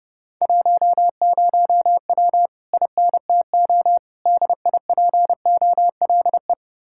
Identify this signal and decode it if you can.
Morse code, heard as a keyed tone, about 30 wpm, 700 Hz. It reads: 10W INTO DIPOLE